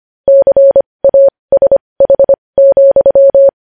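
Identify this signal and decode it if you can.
Morse code, heard as a keyed tone, about 25 wpm, spaced as for 20 wpm, 550 Hz. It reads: CASH,
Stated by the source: Morse news headlines practice at 20 wpm.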